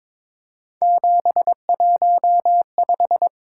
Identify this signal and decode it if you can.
Morse code, heard as a keyed tone, about 21 wpm, 700 Hz. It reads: 715
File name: practice / 22wpm / Three Letter-Number Combinations - 22wpm